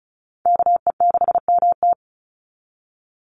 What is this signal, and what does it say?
Morse code, heard as a keyed tone, about 34 wpm, 700 Hz. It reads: KE6MT
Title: NA SOTA Chaser Callsigns 35wpm